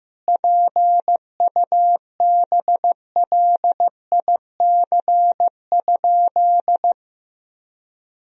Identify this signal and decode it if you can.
Morse code, heard as a keyed tone, about 15 wpm, 700 Hz. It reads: PUBLIC?
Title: Sentences from Top 1000 Words - Review Part 3 - 15wpm